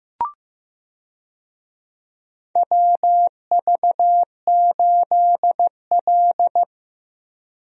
Morse code, heard as a keyed tone, about 15 wpm, 700 Hz. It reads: WV8L
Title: USA Call Signs - 4 Characters - Repeated 2x - 15wpm